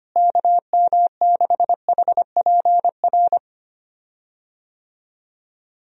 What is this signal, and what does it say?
Morse code, heard as a keyed tone, about 25 wpm, 700 Hz. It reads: KM6HPR
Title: USA Call Signs - 6 Characters - Repeated 3x - 25wpm